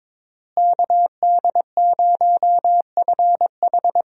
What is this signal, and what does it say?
Morse code, heard as a keyed tone, about 22 wpm, 700 Hz. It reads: KD0FH
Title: USA Call Signs - 5 Characters - Repeated 2x - 22wpm